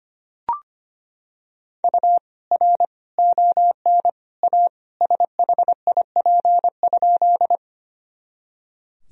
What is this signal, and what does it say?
Morse code, heard as a keyed tone, about 25 wpm, 700 Hz. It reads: U R ON A SHIP?
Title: Sentences from Top 400 Words and Abbreviations 25wpm